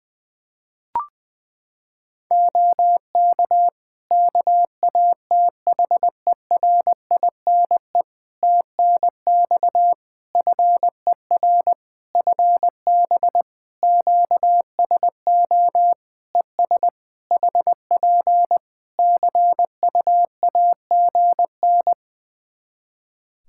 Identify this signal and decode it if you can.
Morse code, heard as a keyed tone, about 21 wpm, 700 Hz. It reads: OK KATHERINE TNX FER FB QSO ES HP CUAGN
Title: K1USN SST QSO Practice 20wpm